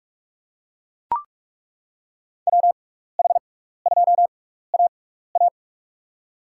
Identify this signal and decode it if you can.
Morse code, heard as a keyed tone, about 42 wpm, spaced as for 15 wpm, 700 Hz. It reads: WH2AA
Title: USA Call Signs - 6x Character Spacing - 45wpm